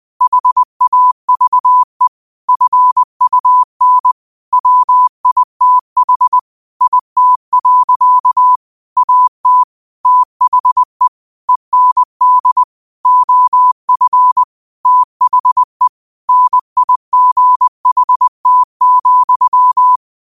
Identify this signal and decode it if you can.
Morse code, heard as a keyed tone, about 20 wpm, 1000 Hz. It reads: HAVE FUN WITH IT. AT THE END OF THE NIGHT,
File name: qotdfiles / 20wpm